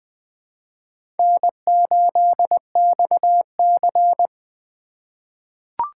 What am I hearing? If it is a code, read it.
Morse code, heard as a keyed tone, about 20 wpm, 700 Hz. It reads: N8XC
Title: USA Call Signs - 4 Characters - 3x Character Spacing - 20wpm